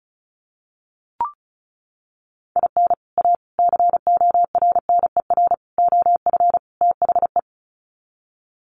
Morse code, heard as a keyed tone, about 34 wpm, 700 Hz. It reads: IN A CORNER OF THE
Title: Sets of 5 Words Encore2 35wpm